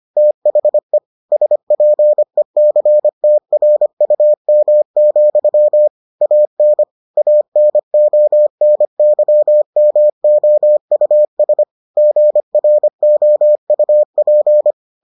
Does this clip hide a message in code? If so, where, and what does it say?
Yes: Morse code from 0.2 to 14.7 s, heard as a keyed tone, about 25 wpm, 600 Hz. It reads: THE SPECTRUM, AN ANONYMOUS GROUP